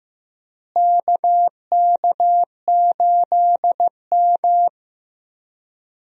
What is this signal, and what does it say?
Morse code, heard as a keyed tone, about 15 wpm, 700 Hz. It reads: KK8M
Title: USA Call Signs - 4 Characters - Repeated 3x - 15wpm